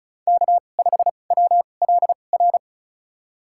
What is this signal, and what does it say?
Morse code, heard as a keyed tone, about 36 wpm, spaced as for 25 wpm, 700 Hz. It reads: K5WLR
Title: USA Call Signs - 5 Characters - 2x Character Spacing - 35wpm